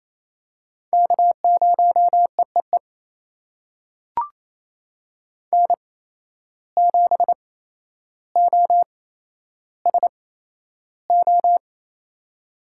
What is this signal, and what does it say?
Morse code, heard as a keyed tone, about 28 wpm, spaced as for 12 wpm, 700 Hz. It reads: K0EEE N7OSO
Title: USA Call Signs - 5 Characters - 8x Character Spacing - 28wpm